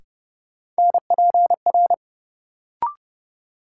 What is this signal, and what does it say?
Morse code, heard as a keyed tone, about 31 wpm, 700 Hz. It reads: NPR
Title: Sets of 5 Words - New Word Drill - 30wpm